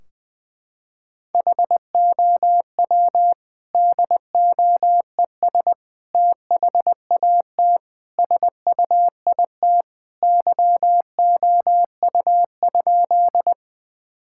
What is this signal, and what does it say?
Morse code, heard as a keyed tone, about 20 wpm, 700 Hz. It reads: HOW DOES THAT SUIT YOU?
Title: Sentences from Top 1600 Words 20wpm